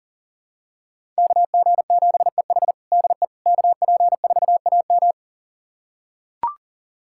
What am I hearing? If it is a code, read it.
Morse code, heard as a keyed tone, about 37 wpm, 700 Hz. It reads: KG7EH DE KP4AM
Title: USA Callsign DE Callsign - 5 Characters - 40wpm